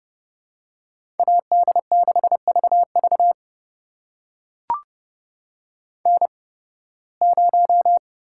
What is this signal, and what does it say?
Morse code, heard as a keyed tone, about 29 wpm, spaced as for 19 wpm, 700 Hz. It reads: AD6VV N0
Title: USA Call Signs - 5 Characters - 8x Character Spacing - 30wpm